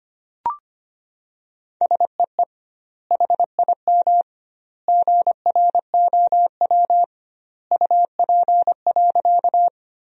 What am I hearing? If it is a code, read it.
Morse code, heard as a keyed tone, about 25 wpm, 700 Hz. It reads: SEE HIM GROW UP.
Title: 2x Word-Spacing - Sentences from Top 300 Words - 25wpm